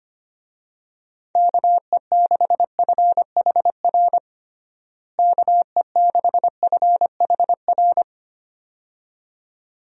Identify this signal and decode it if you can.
Morse code, heard as a keyed tone, about 25 wpm, 700 Hz. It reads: KE6FHR KE6FHR
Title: USA Call Signs - 6 Characters - Repeated 2x - 25wpm